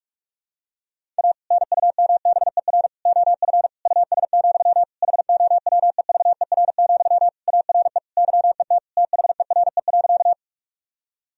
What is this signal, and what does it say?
Morse code, heard as a keyed tone, about 46 wpm, 700 Hz. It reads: A NUMBER OF US, HOWEVER, ARE YET THERE.
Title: Sentences from Top 1000 Words - Review Part 1 - 45wpm